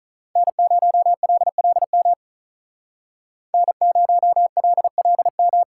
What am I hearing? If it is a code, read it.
Morse code, heard as a keyed tone, about 46 wpm, 700 Hz. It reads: N0LLM N0LLM